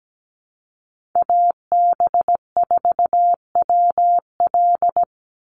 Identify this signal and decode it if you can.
Morse code, heard as a keyed tone, about 17 wpm, 700 Hz. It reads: AB4WL
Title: NA SOTA Chaser Callsigns 17wpm